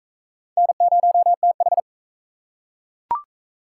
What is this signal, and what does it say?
Morse code, heard as a keyed tone, about 38 wpm, 700 Hz. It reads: N0TH